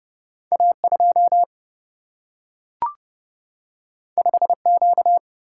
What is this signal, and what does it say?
Morse code, heard as a keyed tone, about 30 wpm, 700 Hz. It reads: A2 5Q
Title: Single-Letter Single-Number 30wpm